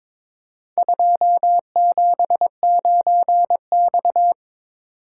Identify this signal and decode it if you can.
Morse code, heard as a keyed tone, about 22 wpm, 700 Hz. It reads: 279X
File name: practice / 22wpm / Four Letter-Number Combinations - 22wpm